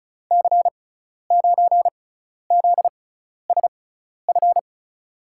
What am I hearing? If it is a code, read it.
Morse code, heard as a keyed tone, about 36 wpm, spaced as for 12 wpm, 700 Hz. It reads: C9ZSF
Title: USA Call Signs - 6 Characters - 6x Character Spacing - 35wpm